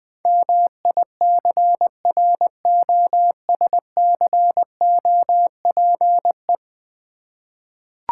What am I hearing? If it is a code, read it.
Morse code, heard as a keyed tone, about 20 wpm, 700 Hz. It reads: MICROSCOPE
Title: Easy Words with Hints - 20wpm